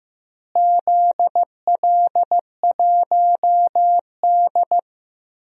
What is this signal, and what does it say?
Morse code, heard as a keyed tone, about 15 wpm, 700 Hz. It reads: ZL1D
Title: DXCC-100 Call Signs - 4 Characters - Repeated 3x - 15wpm